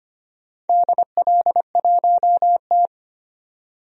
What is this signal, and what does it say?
Morse code, heard as a keyed tone, about 25 wpm, 700 Hz. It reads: DL1T
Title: DXCC-100 Call Signs - 4 Characters - 25wpm